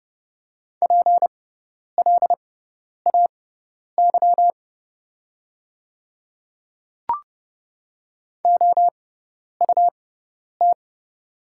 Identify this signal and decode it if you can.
Morse code, heard as a keyed tone, about 30 wpm, spaced as for 10 wpm, 700 Hz. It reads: PLAY OUT
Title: 6x Character Spacing - Top 300 Words - 30wpm